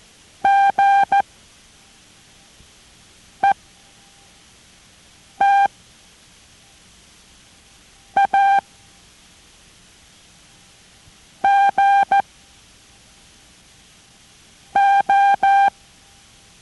Morse code, heard as a keyed tone, about 14 wpm, 790 Hz. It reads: G E T A G O